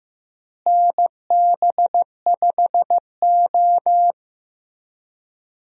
Morse code, heard as a keyed tone, about 15 wpm, 700 Hz. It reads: NB5O
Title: USA Call Signs - 4 Characters - Repeated 3x - 15wpm